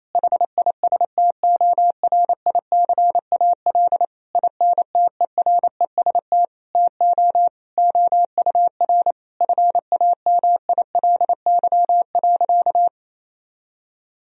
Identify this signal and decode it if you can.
Morse code, heard as a keyed tone, about 28 wpm, 700 Hz. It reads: HISTORICAL INTEREST TO OUR FAMILY.
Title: Sentences from Top 1000 Words - Review Part 4 - 28wpm